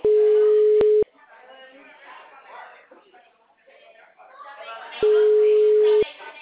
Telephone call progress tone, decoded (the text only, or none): ringback tone